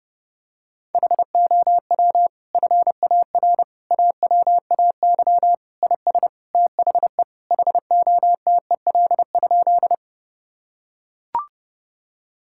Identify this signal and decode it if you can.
Morse code, heard as a keyed tone, about 32 wpm, 700 Hz. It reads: HOW FAR AWAY IS THE HOTEL?
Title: Sentences from Top 1600 Words 30wpm